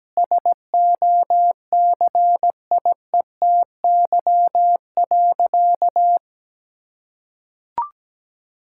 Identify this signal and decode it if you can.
Morse code, heard as a keyed tone, about 17 wpm, 700 Hz. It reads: SOCIETY.
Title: Sentences from Top 1000 Words - Review Part 6 - 17wpm